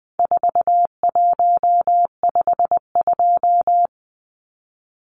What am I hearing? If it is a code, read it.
Morse code, heard as a keyed tone, about 20 wpm, 700 Hz. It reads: 4152
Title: Numbers - 4-Digit 20wpm